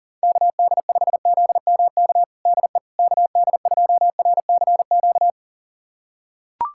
Morse code, heard as a keyed tone, about 40 wpm, 700 Hz. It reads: KD5ZMK DE KD2RCQ
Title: USA Callsign DE Callsign - 6 Characters - 40wpm